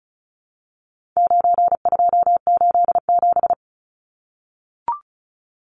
Morse code, heard as a keyed tone, about 34 wpm, 700 Hz. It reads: 9287
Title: Numbers - 4-Digit 35wpm